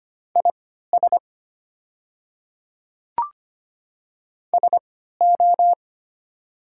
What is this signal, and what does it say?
Morse code, heard as a keyed tone, about 25 wpm, spaced as for 14 wpm, 700 Hz. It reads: IS SO
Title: 3x Character Spacing - Top 300 Words - 25wpm